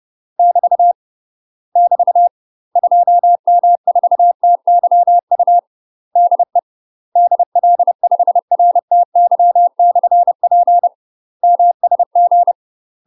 Morse code, heard as a keyed tone, about 30 wpm, 700 Hz. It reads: <BT> <BT> 2M4TYU DE DL5RTY/P MSG
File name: UK MASTER QSOs TRAINING / 30WPM / UK MASTER QSOs 0045